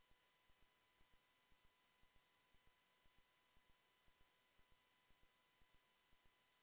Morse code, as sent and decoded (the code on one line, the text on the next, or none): none